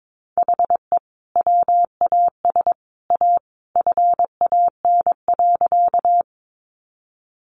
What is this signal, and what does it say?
Morse code, heard as a keyed tone, about 22 wpm, 700 Hz. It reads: HE WAS A FAN.
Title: Sentences from Top 1300 Words 22wpm